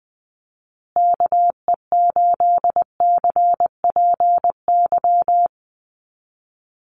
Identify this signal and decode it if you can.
Morse code, heard as a keyed tone, about 20 wpm, 700 Hz. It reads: KE8CPY
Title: USA Call Signs Repeated 3x 20wpm